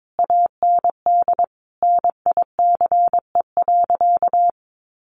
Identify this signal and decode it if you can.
Morse code, heard as a keyed tone, about 22 wpm, 700 Hz. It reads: AND NICE.
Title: Sentences from Top 1200 Words 22wpm